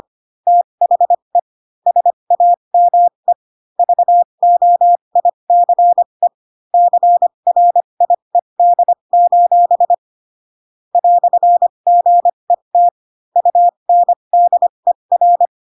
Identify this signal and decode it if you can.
Morse code, heard as a keyed tone, about 25 wpm, spaced as for 22 wpm, 700 Hz. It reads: THE SAME VOICE CRIED: "GET UNDER
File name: Pinocchio / Pinocchio ch6 18 25 ik1hge